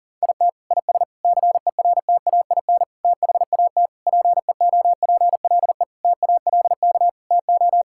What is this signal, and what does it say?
Morse code, heard as a keyed tone, about 40 wpm, 700 Hz. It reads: IT IS CERTAIN THAT PEOPLE TALK TO
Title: Sentences from Top 1000 Words - Review Part 5 - 40wpm